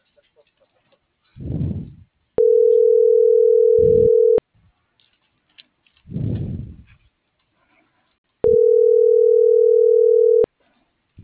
Ringback tone (North America), 440 + 480 Hz.